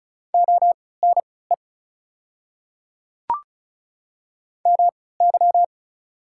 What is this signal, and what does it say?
Morse code, heard as a keyed tone, about 34 wpm, spaced as for 20 wpm, 700 Hz. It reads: ONE MY
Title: 3x Character Spacing - Top 100 Words - 35wpm